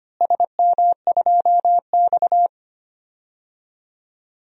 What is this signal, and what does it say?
Morse code, heard as a keyed tone, about 25 wpm, 700 Hz. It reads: SM2X
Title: DXCC-100 Call Signs - 4 Characters - Repeated 3x - 25wpm